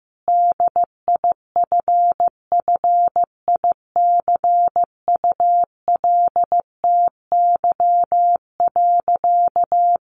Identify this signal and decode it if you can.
Morse code, heard as a keyed tone, about 15 wpm, 700 Hz. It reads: DIFFICULTY.
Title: Sentences from Top 1500 Words 15wpm